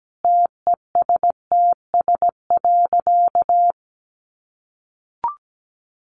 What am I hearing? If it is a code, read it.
Morse code, heard as a keyed tone, about 17 wpm, 700 Hz. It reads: TESTS.
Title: Additional Easy Phrases 17wpm